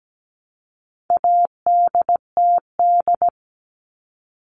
Morse code, heard as a keyed tone, about 17 wpm, 700 Hz. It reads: ADTD
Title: All Cut Numbers - 4-Digit 17wpm